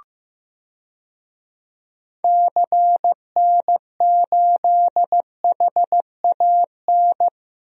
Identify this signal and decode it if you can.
Morse code, heard as a keyed tone, about 16 wpm, 700 Hz. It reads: CN8HAN
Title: DXCC-100 Call Signs - 6 Characters - Repeated 2x - 15wpm